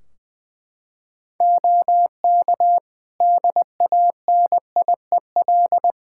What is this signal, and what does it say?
Morse code, heard as a keyed tone, about 20 wpm, 700 Hz. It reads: OK DANIEL
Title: K1USN SST QSO Practice 20wpm